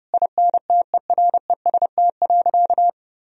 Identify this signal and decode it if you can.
Morse code, heard as a keyed tone, about 30 wpm, 700 Hz. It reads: INTEREST.
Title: Sentences from Top 1000 Words - Review Part 4 - 30wpm